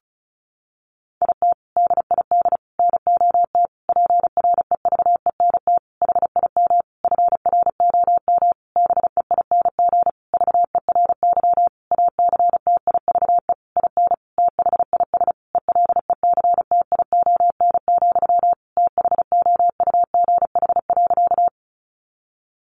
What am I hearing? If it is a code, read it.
Morse code, heard as a keyed tone, about 34 wpm, 700 Hz. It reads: IT DID NOT PREVENT HIM FROM BEING VERY ACTIVE IN THIS ELECTION, THOUGH.
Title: Sentences from Top 1500 Words 35wpm